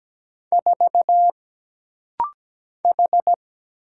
Morse code, heard as a keyed tone, about 17 wpm, 700 Hz. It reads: 4 H
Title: Single Letter-Number - Rapid-Fire 17wpm